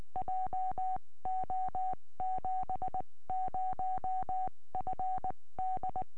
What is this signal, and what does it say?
Morse code, heard as a keyed tone, about 20 wpm, spaced as for 16 wpm, 750 Hz. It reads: JO70FD